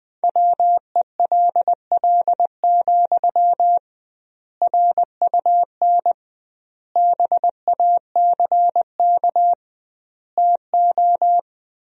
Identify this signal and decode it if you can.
Morse code, heard as a keyed tone, about 20 wpm, 700 Hz. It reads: WELL, RUN BACK TO
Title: 2x Word-Spacing - Sentences from Top 300 Words - 20wpm